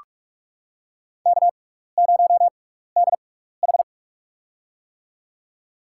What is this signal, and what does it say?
Morse code, heard as a keyed tone, about 46 wpm, spaced as for 15 wpm, 700 Hz. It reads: K0DH
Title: USA Call Signs - 6x Character Spacing - 45wpm